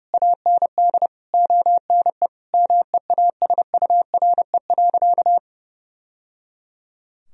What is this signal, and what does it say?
Morse code, heard as a keyed tone, about 30 wpm, 700 Hz. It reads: AND ONE MEASURE.
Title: Sentences from Top 1000 Words - Review Part 3 - 30wpm